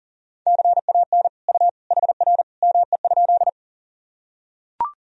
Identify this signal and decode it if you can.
Morse code, heard as a keyed tone, about 40 wpm, 700 Hz. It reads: CAN U HR ME?
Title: Sentences from Top 300 Words and Abbreviations 40wpm